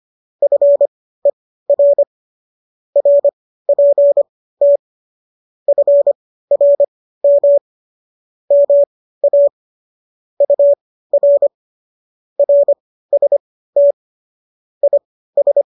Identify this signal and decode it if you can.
Morse code, heard as a keyed tone, about 25 wpm, spaced as for 15 wpm, 570 Hz. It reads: FER RPT FRM MA UR RST IS